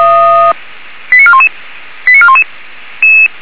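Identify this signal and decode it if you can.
Five-tone selective call (ZVEI), heard as quick runs of five tones, tones of about 70 ms. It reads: 89411 89411
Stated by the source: SDR-received ZVEI five-tone selective call, from the Feuerwehr Orlamünde audio archive.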